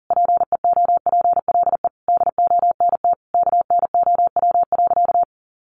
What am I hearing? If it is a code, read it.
Morse code, heard as a keyed tone, about 40 wpm, 700 Hz. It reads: PEOPLE DONT KNOW.
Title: Sentences from Top 600 Words 40wpm